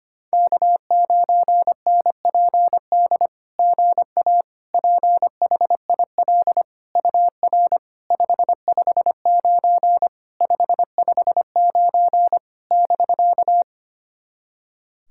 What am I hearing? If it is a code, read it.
Morse code, heard as a keyed tone, about 25 wpm, 660 Hz. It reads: K9NPD GA PHIL UR 559 559 <BK>
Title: SOTA QSO Practice 25wpm